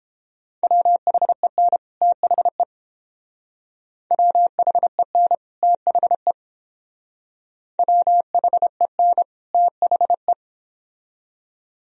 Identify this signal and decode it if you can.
Morse code, heard as a keyed tone, about 30 wpm, 700 Hz. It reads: WHEN THE WHEN THE WHEN THE